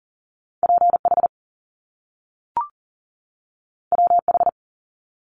Morse code, heard as a keyed tone, about 40 wpm, 700 Hz. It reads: PH WH
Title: Top Two Letter Combinations in Words 40wpm